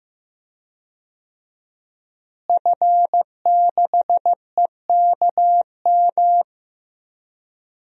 Morse code, heard as a keyed tone, about 16 wpm, 700 Hz. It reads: F6EKM
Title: DXCC-100 Call Signs - 5 Characters - Repeated 2x - 15wpm